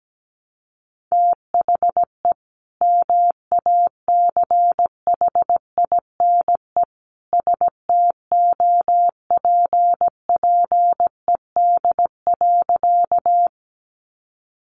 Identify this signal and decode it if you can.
Morse code, heard as a keyed tone, about 17 wpm, 700 Hz. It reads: THE MACHINE STOPPED.
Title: A Sound Of Thunder 17wpm